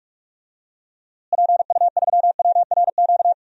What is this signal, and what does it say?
Morse code, heard as a keyed tone, about 42 wpm, 700 Hz. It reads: PU3WRQ
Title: DXCC-100 Call Signs - 6 Characters - Repeated 2x - 45wpm